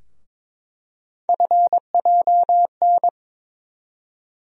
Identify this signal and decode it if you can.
Morse code, heard as a keyed tone, about 22 wpm, 700 Hz. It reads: FJN
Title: USA Call Sign Postfixes - 3 Letters 22wpm